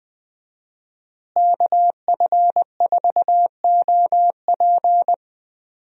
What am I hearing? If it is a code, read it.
Morse code, heard as a keyed tone, about 20 wpm, 700 Hz. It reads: KF4OP
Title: USA Call Signs - 5 Characters - Repeated 3x - 20wpm